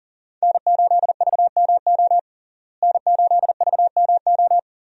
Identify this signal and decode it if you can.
Morse code, heard as a keyed tone, about 40 wpm, 700 Hz. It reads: N8VMO N8VMO
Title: USA Call Signs - 5 Characters - Repeated 3x - 40wpm